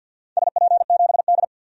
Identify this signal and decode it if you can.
Morse code, heard as a keyed tone, about 51 wpm, 700 Hz. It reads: SP7D
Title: DXCC-100 Call Signs - 4 Characters - 50wpm